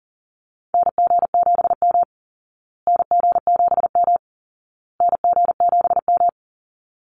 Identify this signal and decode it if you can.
Morse code, heard as a keyed tone, about 37 wpm, 700 Hz. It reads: NG7M NG7M NG7M
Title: Call Signs Repeated 2x - 40wpm